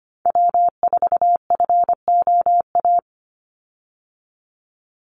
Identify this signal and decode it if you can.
Morse code, heard as a keyed tone, about 25 wpm, 700 Hz. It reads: W4FOA
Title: Call Signs Repeated 3x - 25wpm